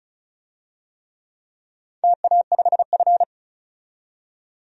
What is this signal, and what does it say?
Morse code, heard as a keyed tone, about 36 wpm, 700 Hz. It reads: TA5F
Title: DXCC-100 Call Signs - 4 Characters - 35wpm